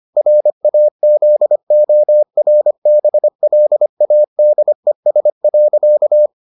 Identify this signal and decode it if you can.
Morse code, heard as a keyed tone, about 25 wpm, 600 Hz. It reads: RAZORBLADES.